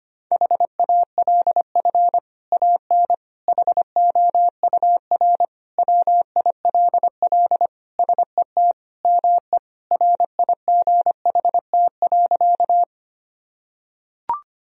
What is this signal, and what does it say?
Morse code, heard as a keyed tone, about 25 wpm, 700 Hz. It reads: HALF AN HOUR WILL SET ME RIGHT.
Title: Sentences from Top 1000 Words - Review Part 1 - 25wpm